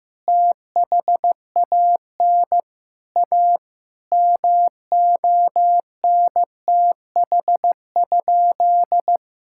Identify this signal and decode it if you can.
Morse code, heard as a keyed tone, about 15 wpm, 700 Hz. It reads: THAN A MONTH?
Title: Sentences from Top 1000 Words - Review Part 5 - 15wpm